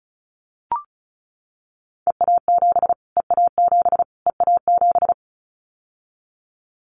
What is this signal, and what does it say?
Morse code, heard as a keyed tone, about 34 wpm, 700 Hz. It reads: EA7 EA7 EA7
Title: Callsign Prefixes for Top 25 DXCC - Repeated 3x - 35wpm